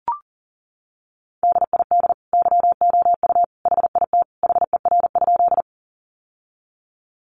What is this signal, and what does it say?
Morse code, heard as a keyed tone, about 37 wpm, 700 Hz. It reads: DID YOU HIT HER?
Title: Sentences from Top 1200 Words 40wpm